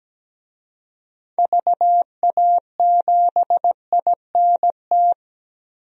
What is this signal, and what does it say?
Morse code, heard as a keyed tone, about 17 wpm, 700 Hz. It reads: VA7INT
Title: DXCC-100 Call Signs - 6 Characters - Repeated 3x - 17wpm